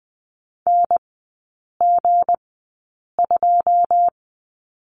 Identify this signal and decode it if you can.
Morse code, heard as a keyed tone, about 20 wpm, spaced as for 8 wpm, 690 Hz. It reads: NG2
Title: 191 Lesson 31e - Letter v - Three Characters 20wpm